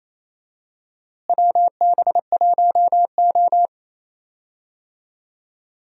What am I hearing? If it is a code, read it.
Morse code, heard as a keyed tone, about 28 wpm, 700 Hz. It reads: WB1O
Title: USA Call Signs - 4 Characters - Repeated 3x - 28wpm